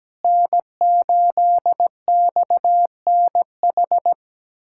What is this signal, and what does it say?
Morse code, heard as a keyed tone, about 17 wpm, 700 Hz. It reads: N8XNH
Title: USA Callsign DE Callsign - 5 Characters - 17wpm